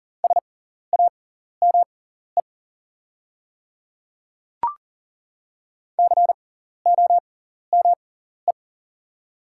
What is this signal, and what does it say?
Morse code, heard as a keyed tone, about 40 wpm, spaced as for 14 wpm, 690 Hz. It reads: SAME COME
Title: 6x Character Spacing - Top 200 Words - 40wpm